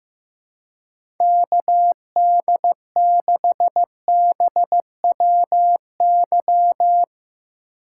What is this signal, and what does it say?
Morse code, heard as a keyed tone, about 15 wpm, 700 Hz. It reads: KD6BWY
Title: USA Call Signs - 3x Character Spacing - 15wpm